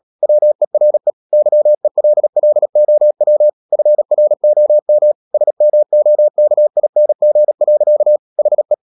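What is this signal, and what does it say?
Morse code, heard as a keyed tone, about 37 wpm, 600 Hz. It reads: WERE YELLOW FROM SMOKING. HE
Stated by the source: Morse code practice at 35–39 wpm sending English text of Ken Follett's novel Never, from CW med Gnister.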